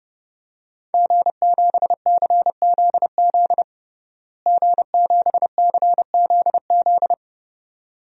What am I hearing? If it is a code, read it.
Morse code, heard as a keyed tone, about 32 wpm, 700 Hz. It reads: G7CZZ G7CZZ
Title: DXCC-100 Call Signs - 5 Characters - Repeated 2x - 30wpm